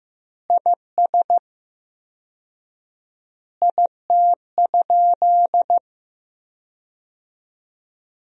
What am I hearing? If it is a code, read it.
Morse code, heard as a keyed tone, about 15 wpm, 700 Hz. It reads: IS IT?
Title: 4x Word-Spacing - Sentences from Top 200 Words - 15wpm